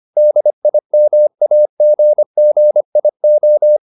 Morse code, heard as a keyed tone, about 25 wpm, 600 Hz. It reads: DIMAGGIO
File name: the old man and the sea part 1 25wpm